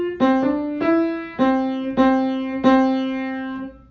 There is no code.